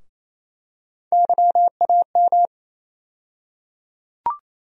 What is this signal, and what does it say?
Morse code, heard as a keyed tone, about 28 wpm, 700 Hz. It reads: YAM
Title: USA Call Sign Postfixes - 3 Letters 28wpm